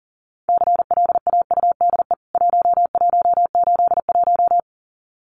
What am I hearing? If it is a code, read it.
Morse code, heard as a keyed tone, about 40 wpm, 700 Hz. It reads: CLAUDE 1181
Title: CWT Exchange 40wpm